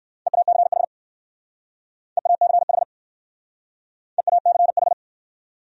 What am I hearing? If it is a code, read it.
Morse code, heard as a keyed tone, about 57 wpm, 700 Hz. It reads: EACH EACH EACH